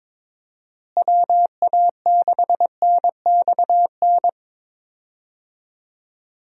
Morse code, heard as a keyed tone, about 21 wpm, 700 Hz. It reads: WA6NXN
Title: USA Call Signs - 6 Characters - Repeated 2x - 22wpm